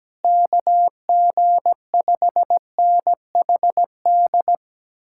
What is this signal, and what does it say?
Morse code, heard as a keyed tone, about 17 wpm, 700 Hz. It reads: KG5NHD
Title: Speed-Racing Common QSO Phrases Encore - 17wpm